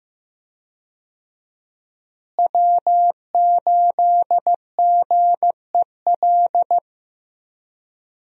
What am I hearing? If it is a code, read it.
Morse code, heard as a keyed tone, about 15 wpm, 700 Hz. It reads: W8GEL
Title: USA Call Signs - 5 Characters - Repeated 2x - 15wpm